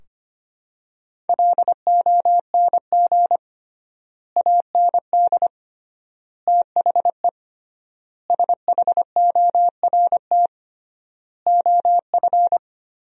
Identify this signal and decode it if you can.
Morse code, heard as a keyed tone, about 25 wpm, 700 Hz. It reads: LONG AND THE SHORT OF
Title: 3x Word-Spacing - Sentences from Top 400 Words - 25wpm